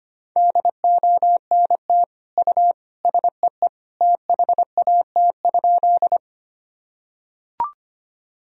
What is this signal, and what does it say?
Morse code, heard as a keyed tone, about 25 wpm, 700 Hz. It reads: DONT U SEE THAT?
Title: Sentences from Top 300 Words and Abbreviations 25wpm